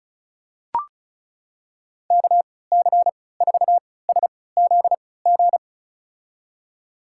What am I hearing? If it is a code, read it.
Morse code, heard as a keyed tone, about 36 wpm, spaced as for 20 wpm, 700 Hz. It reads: KC4SZG
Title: USA Call Signs - 6 Characters - 3x Character Spacing - 35wpm